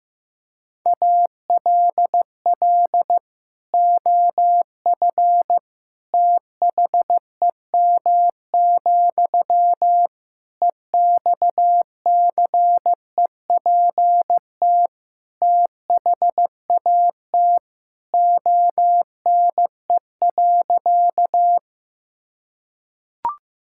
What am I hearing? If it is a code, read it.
Morse code, heard as a keyed tone, about 15 wpm, 700 Hz. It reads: ALL OF THEM, EXCEPT THAT ONE.
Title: Sentences from Top 1600 Words 15wpm